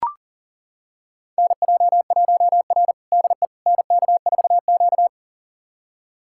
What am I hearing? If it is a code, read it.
Morse code, heard as a keyed tone, about 40 wpm, 700 Hz. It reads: NJ1R DE NK4Q
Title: USA Callsign DE Callsign - 4 Characters - 40wpm